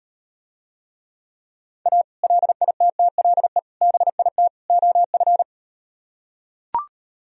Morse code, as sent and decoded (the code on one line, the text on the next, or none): .- / .-.. .. - - .-.. / -... .. - / --- ..-.
A LITTLE BIT OF